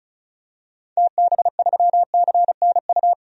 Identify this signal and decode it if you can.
Morse code, heard as a keyed tone, about 36 wpm, 690 Hz. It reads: TB3CNU
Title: DXCC-100 Call Signs - 6 Characters - 35wpm